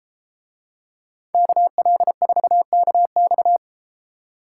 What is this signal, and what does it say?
Morse code, heard as a keyed tone, about 33 wpm, 700 Hz. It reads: KL4KX